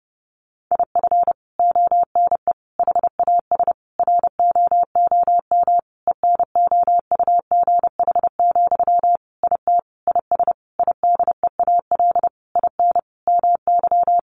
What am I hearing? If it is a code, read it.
Morse code, heard as a keyed tone, about 30 wpm, 700 Hz. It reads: IF ONE HAS ROOM ENOUGH, IT IS IDEAL IN MY
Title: Sentences from Top 1400 Words 30wpm